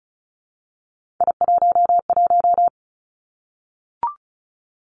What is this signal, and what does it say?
Morse code, heard as a keyed tone, about 36 wpm, 700 Hz. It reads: I11
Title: Three Challenging Letter-Number Combinations 35wpm